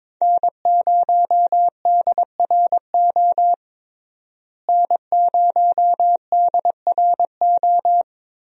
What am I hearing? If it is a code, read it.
Morse code, heard as a keyed tone, about 22 wpm, 700 Hz. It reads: N0DRO N0DRO
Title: USA Call Signs - 5 Characters - Repeated 3x - 22wpm